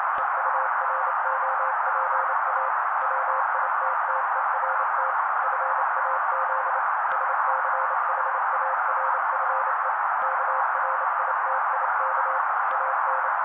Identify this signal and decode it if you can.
Morse code, heard as a keyed tone, about 28 wpm, 530 Hz. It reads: EUROPA WITTERT FAZ SCHARFE KRITIK AN
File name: daily / 27wpm noise